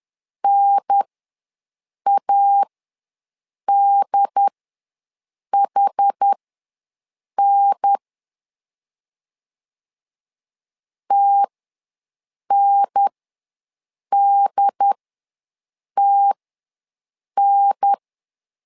Morse code, heard as a keyed tone, about 10 wpm, spaced as for 6 wpm, 800 Hz. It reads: NADHN TNDTN
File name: 2 UHDN ws 10wpm